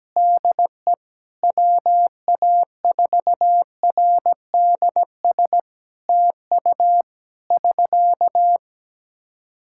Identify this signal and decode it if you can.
Morse code, heard as a keyed tone, about 17 wpm, 700 Hz. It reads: DE WA4RDS TU <SK>
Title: Practice QSO - Exchange 4 - Ending - 17wpm